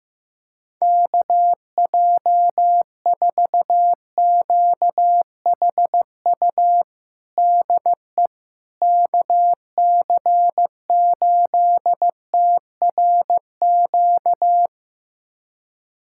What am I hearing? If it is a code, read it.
Morse code, heard as a keyed tone, about 15 wpm, 700 Hz. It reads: KJ4QHU DE KC8TRQ
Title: USA Callsign DE Callsign 15wpm